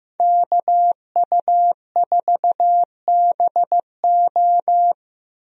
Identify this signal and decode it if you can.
Morse code, heard as a keyed tone, about 15 wpm, 700 Hz. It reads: KU4BO
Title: USA Call Signs - 5 Characters - 15wpm